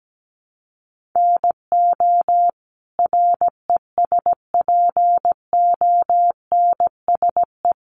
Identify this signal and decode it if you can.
Morse code, heard as a keyed tone, about 17 wpm, 700 Hz. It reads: NO RESPONSE
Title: Sentences from Top 1400 Words 17wpm